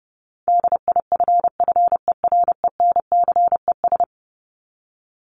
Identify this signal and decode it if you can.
Morse code, heard as a keyed tone, about 30 wpm, 700 Hz. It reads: DIFFERENCES
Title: ES:S Suffix - Top 100 Words with Base Words 30wpm